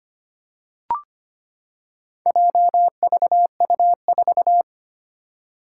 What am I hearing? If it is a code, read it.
Morse code, heard as a keyed tone, about 25 wpm, 700 Hz. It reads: JVU4
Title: Four Challenging Letter-Number Combinations 25wpm